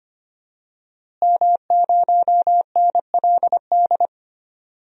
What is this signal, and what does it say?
Morse code, heard as a keyed tone, about 25 wpm, 700 Hz. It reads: M0NLD
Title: DXCC-100 Call Signs - 5 Characters - Repeated 2x - 25wpm